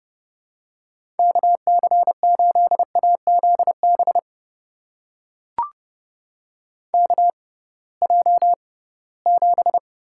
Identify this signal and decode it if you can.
Morse code, heard as a keyed tone, about 31 wpm, spaced as for 20 wpm, 700 Hz. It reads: KC8AZB KJ7
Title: USA Call Signs - 6x Character Spacing - 30wpm